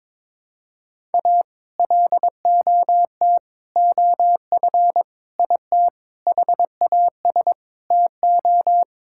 Morse code, heard as a keyed tone, about 22 wpm, 700 Hz. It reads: A LOT OF IT HAS TO
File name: Sentences with 9-Word Phrases - 22wpm